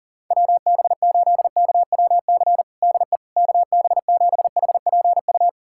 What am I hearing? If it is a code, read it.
Morse code, heard as a keyed tone, about 37 wpm, 700 Hz. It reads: WB8KWC DE KB7HPU
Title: USA Callsign DE Callsign - 6 Characters - 40wpm